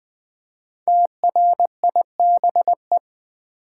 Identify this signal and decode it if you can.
Morse code, heard as a keyed tone, about 20 wpm, 700 Hz. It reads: TRIBE
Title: Build A Word - 1 Letter At A Time - 5-Letter Words - 20wpm